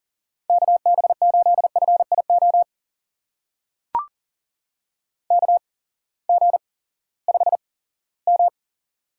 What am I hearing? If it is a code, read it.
Morse code, heard as a keyed tone, about 42 wpm, spaced as for 20 wpm, 700 Hz. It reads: KB8FIO KG5M
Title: USA Call Signs - 6 Characters - 8x Character Spacing - 40wpm